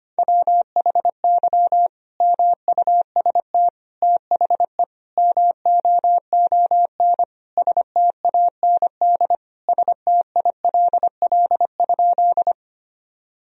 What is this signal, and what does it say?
Morse code, heard as a keyed tone, about 25 wpm, 700 Hz. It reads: WHY MUST THE MOON STAND STILL?
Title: Sentences from Top 1000 Words - Review Part 3 - 25wpm